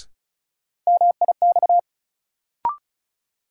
Morse code, heard as a keyed tone, about 34 wpm, 700 Hz. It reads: MIX
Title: Sentences from Top 1100 Words - New Word Drill Alt - 35wpm